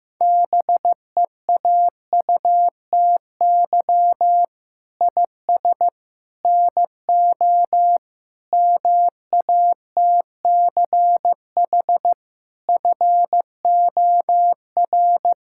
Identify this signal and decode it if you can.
Morse code, heard as a keyed tone, about 15 wpm, 700 Hz. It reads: BEAUTY IS NO MATCH FOR
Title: Sentences from Top 1600 Words 15wpm